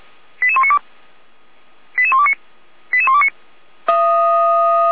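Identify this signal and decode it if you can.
Five-tone selective call (ZVEI), heard as quick runs of five tones, tones of about 70 ms. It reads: 88192 88129 88129
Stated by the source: ZVEI five-tone selective call received on SDR, from the Feuerwehr Orlamünde (fire brigade) audio server.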